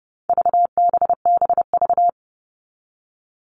Morse code, heard as a keyed tone, about 30 wpm, 700 Hz. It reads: VBBV
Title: All Cut Numbers - 4-Digit 30wpm